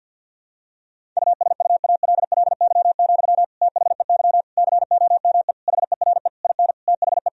Morse code, heard as a keyed tone, about 50 wpm, 700 Hz. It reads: USUALLY, THEY COME HERE IN THE
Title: Sentences from Top 1000 Words - Review Part 3 - 50wpm